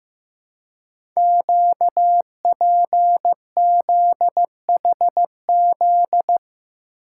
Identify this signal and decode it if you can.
Morse code, heard as a keyed tone, about 15 wpm, 700 Hz. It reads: QPZHZ
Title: Code Groups - 5 Letters - 15wpm